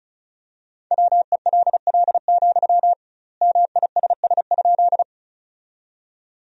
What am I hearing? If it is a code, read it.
Morse code, heard as a keyed tone, about 34 wpm, 700 Hz. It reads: WELL, MISS?
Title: 2x Word-Spacing - Sentences from Top 500 Words - 35wpm